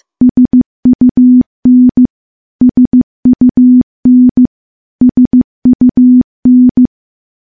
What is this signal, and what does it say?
Morse code, heard as a keyed tone, about 16 wpm, 250 Hz. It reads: SUN SUN SUN
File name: morse / morse-training-15WPM